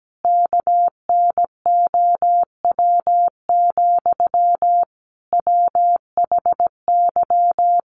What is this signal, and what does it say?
Morse code, heard as a keyed tone, about 17 wpm, 700 Hz. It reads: KNOW, WHY
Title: Sentences from Top 300 Words 17wpm